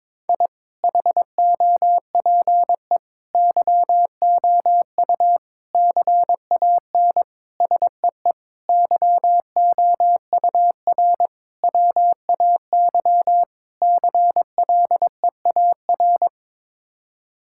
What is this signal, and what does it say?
Morse code, heard as a keyed tone, about 22 wpm, 700 Hz. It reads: I HOPE YOU CAN SEE YOUR WAY CLEAR
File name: practice / 22wpm / Sets of 8 Words - 22wpm